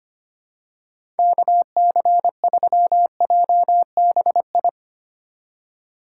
Morse code, heard as a keyed tone, about 25 wpm, 700 Hz. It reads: KC3JBI
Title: USA Call Signs - 4x Character Spacing - 25wpm